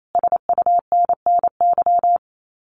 Morse code, heard as a keyed tone, about 28 wpm, 700 Hz. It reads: SUNNY
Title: Common QSO Phrases Encore 28wpm